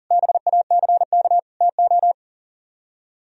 Morse code, heard as a keyed tone, about 52 wpm, 700 Hz. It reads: BACK TO